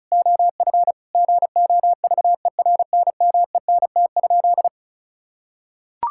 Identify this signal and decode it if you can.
Morse code, heard as a keyed tone, about 36 wpm, 700 Hz. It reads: OF GOVERNMENT?
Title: Sentences from Top 1000 Words - Review Part 5 - 35wpm